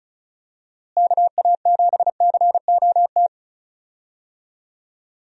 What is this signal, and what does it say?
Morse code, heard as a keyed tone, about 36 wpm, 700 Hz. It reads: KA7COT
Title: USA Call Signs - 6 Characters - Repeated 2x - 35wpm